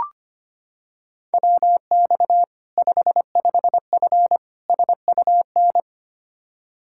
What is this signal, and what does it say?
Morse code, heard as a keyed tone, about 25 wpm, 700 Hz. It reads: WX 55F SUN
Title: Practice QSO - Exchange 2 - Info - 25wpm